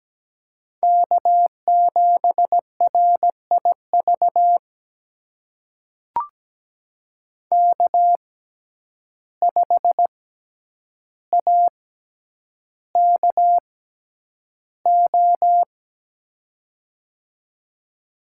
Morse code, heard as a keyed tone, about 17 wpm, spaced as for 9 wpm, 700 Hz. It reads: K7RIV K5AKO
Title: USA Call Signs - 5 Characters - 6x Character Spacing - 17wpm